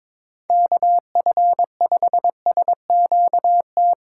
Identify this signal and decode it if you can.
Morse code, heard as a keyed tone, about 21 wpm, 700 Hz. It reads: KF5SQT
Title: USA Call Signs - 3x Character Spacing - 22wpm